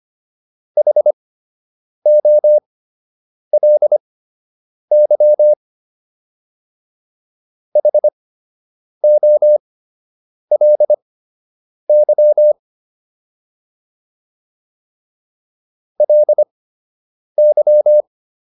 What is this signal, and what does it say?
Morse code, heard as a keyed tone, about 25 wpm, spaced as for 8 wpm, 600 Hz. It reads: HOLY HOLY LY